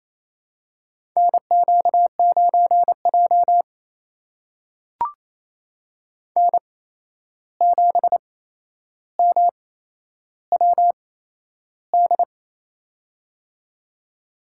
Morse code, heard as a keyed tone, about 28 wpm, spaced as for 11 wpm, 700 Hz. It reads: NQ9J N7MWD
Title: USA Call Signs - 8x Character Spacing - 28wpm